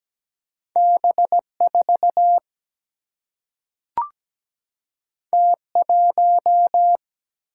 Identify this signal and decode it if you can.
Morse code, heard as a keyed tone, about 17 wpm, 700 Hz. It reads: B4 T1
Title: Single-Letter Single-Number 17wpm